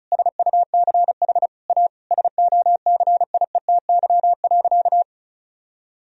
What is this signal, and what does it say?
Morse code, heard as a keyed tone, about 36 wpm, 700 Hz. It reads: SUCH A SOCIETY.
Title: Sentences from Top 1000 Words - Review Part 4 - 35wpm